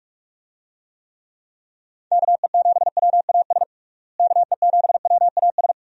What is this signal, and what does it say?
Morse code, heard as a keyed tone, about 46 wpm, 700 Hz. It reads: KE7WAS KE7WAS
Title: USA Call Signs - 6 Characters - Repeated 3x - 45wpm